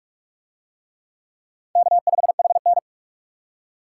Morse code, heard as a keyed tone, about 42 wpm, 700 Hz. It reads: K5HN